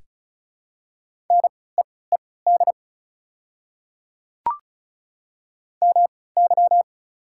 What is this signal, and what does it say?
Morse code, heard as a keyed tone, about 34 wpm, spaced as for 20 wpm, 700 Hz. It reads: NEED MY
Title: 3x Character Spacing - Top 300 Words - 35wpm